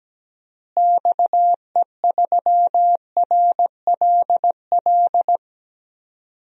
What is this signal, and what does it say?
Morse code, heard as a keyed tone, about 17 wpm, 700 Hz. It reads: XE3RLL
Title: DXCC-100 Call Signs - 6 Characters - Repeated 3x - 17wpm